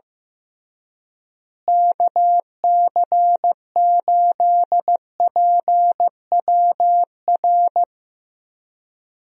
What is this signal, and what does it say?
Morse code, heard as a keyed tone, about 15 wpm, 700 Hz. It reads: KC8PWR
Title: USA Call Signs - 6 Characters - Repeated 3x - 15wpm